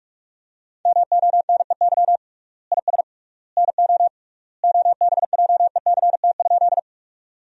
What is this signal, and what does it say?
Morse code, heard as a keyed tone, about 46 wpm, 700 Hz. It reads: MONEY IS NO OBJECT?
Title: 3x Word-Spacing - Sentences from Top 500 Words - 45wpm